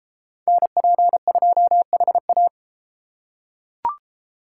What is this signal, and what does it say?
Morse code, heard as a keyed tone, about 33 wpm, 700 Hz. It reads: NP2HA